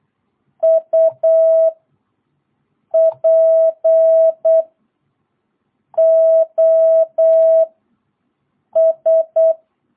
Morse code, heard as a keyed tone, about 8 wpm, 660 Hz. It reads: U P O S
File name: jh8cbh / 2022cw